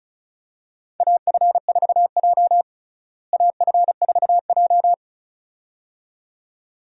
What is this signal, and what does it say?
Morse code, heard as a keyed tone, about 36 wpm, 700 Hz. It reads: AF4J AF4J
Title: USA Call Signs - 4 Characters - Repeated 2x - 35wpm